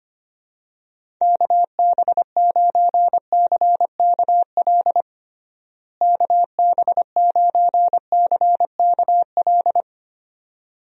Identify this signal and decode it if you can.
Morse code, heard as a keyed tone, about 25 wpm, 700 Hz. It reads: KB9CKL KB9CKL
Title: USA Call Signs - 6 Characters - Repeated 2x - 25wpm